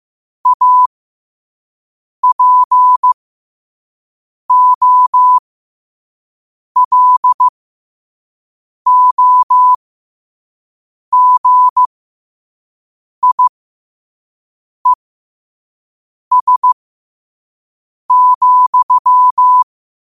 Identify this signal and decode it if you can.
Morse code, heard as a keyed tone, about 15 wpm, spaced as for 5 wpm, 1000 Hz. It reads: APOLOGIES,
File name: qotdfiles / 5wpm